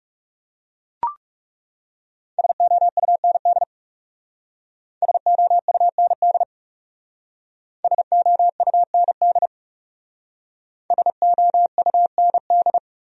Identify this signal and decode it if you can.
Morse code, heard as a keyed tone, about 37 wpm, 700 Hz. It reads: SOUND SOUND SOUND SOUND
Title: Speed-Racing Top 300 Words 30wpm